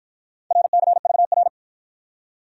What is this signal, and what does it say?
Morse code, heard as a keyed tone, about 54 wpm, 700 Hz. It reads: RCVR